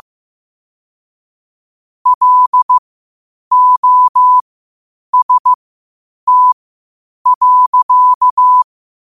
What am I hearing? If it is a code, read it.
Morse code, heard as a keyed tone, about 15 wpm, spaced as for 8 wpm, 1000 Hz. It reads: LOST.
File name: qotdfiles / 8wpm